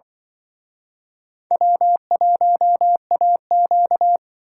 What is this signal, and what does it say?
Morse code, heard as a keyed tone, about 24 wpm, 700 Hz. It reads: W1AQ
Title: Speed Racing - USA 1x2 Call Signs 20wpm